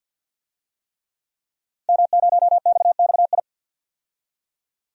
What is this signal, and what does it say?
Morse code, heard as a keyed tone, about 46 wpm, 700 Hz. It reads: M0XXI
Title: DXCC-100 Call Signs - 5 Characters - Repeated 3x - 50wpm